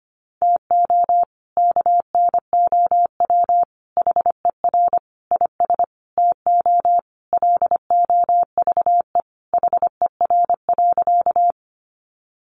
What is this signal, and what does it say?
Morse code, heard as a keyed tone, about 25 wpm, 700 Hz. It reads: TO KNOW HER IS TO LOVE HER.
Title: Sentences from Top 400 Words 25wpm